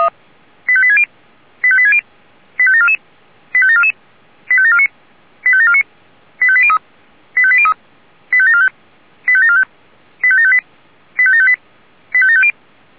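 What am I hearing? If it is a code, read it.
Five-tone selective call (ZVEI), heard as quick runs of five tones, tones of about 70 ms. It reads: 85860 85860 85744 85744 85749 85749 85803 85803 85745 85745 85758 85758 85750
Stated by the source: ZVEI five-tone selective call 85725 received by SDR.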